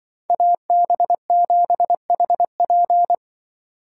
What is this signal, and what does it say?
Morse code, heard as a keyed tone, about 23 wpm, 700 Hz. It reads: AB7HP
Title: Speed Racing - USA 2x2 Call Signs 20wpm